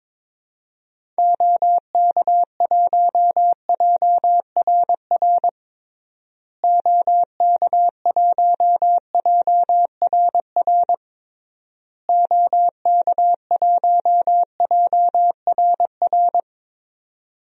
Morse code, heard as a keyed tone, about 22 wpm, 700 Hz. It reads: OK1JRR OK1JRR OK1JRR
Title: DXCC-100 Call Signs - 6 Characters - Repeated 3x - 22wpm